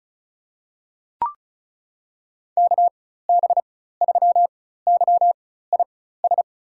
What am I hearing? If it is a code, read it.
Morse code, heard as a keyed tone, about 34 wpm, spaced as for 16 wpm, 700 Hz. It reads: KB3YIS
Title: USA Call Signs - 6 Characters - 4x Character Spacing - 35wpm